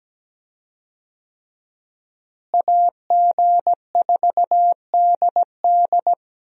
Morse code, heard as a keyed tone, about 17 wpm, 700 Hz. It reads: AG4DD
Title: USA Call Signs - 5 Characters - Repeated 3x - 17wpm